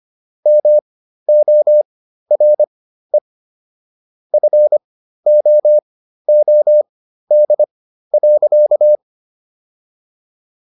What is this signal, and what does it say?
Morse code, heard as a keyed tone, about 25 wpm, spaced as for 13 wpm, 600 Hz. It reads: MORE FOOD.